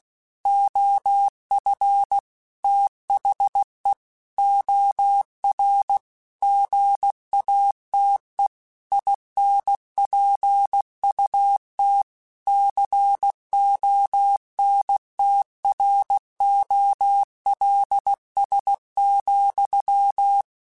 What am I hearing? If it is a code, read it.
Morse code, heard as a keyed tone, about 16 wpm, 800 Hz. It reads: OF THE OR GATE INPUT CONTROLS,
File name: Technical Subjects / Technical Subjects-mp3s / OR Gate @15